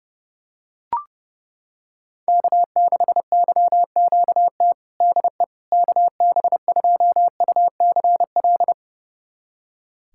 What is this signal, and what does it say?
Morse code, heard as a keyed tone, about 30 wpm, 700 Hz. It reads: K6YQT DE KB2UCL
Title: USA Callsign DE Callsign 30wpm